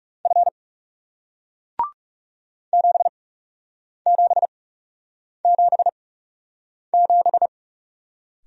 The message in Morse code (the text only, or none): F 7777